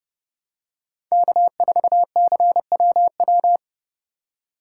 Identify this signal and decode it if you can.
Morse code, heard as a keyed tone, about 30 wpm, 700 Hz. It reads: K4CWW
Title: USA Call Signs - 5 Characters - 4x Character Spacing - 30wpm